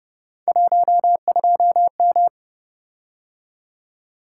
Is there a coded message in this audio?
Yes: Morse code from 0.5 to 2.3 s, heard as a keyed tone, about 29 wpm, 700 Hz. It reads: 12M